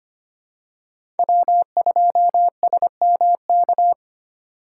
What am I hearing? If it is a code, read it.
Morse code, heard as a keyed tone, about 25 wpm, 700 Hz. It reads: W2SMK
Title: USA Call Signs - 3x Character Spacing - 25wpm